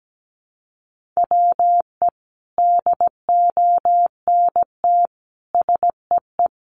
Morse code, heard as a keyed tone, about 17 wpm, 700 Hz. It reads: WE DONT SEE
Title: Sentences from Top 300 Words 17wpm